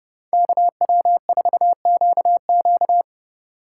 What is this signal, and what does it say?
Morse code, heard as a keyed tone, about 29 wpm, 700 Hz. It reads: KW4QQ